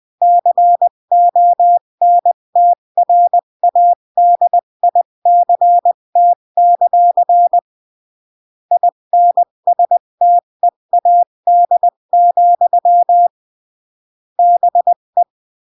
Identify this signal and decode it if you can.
Morse code, heard as a keyed tone, about 20 wpm, 700 Hz. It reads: CONTRADICT; INSTEAD, BE
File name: CW / jodorowsky / 82 commandments 20-15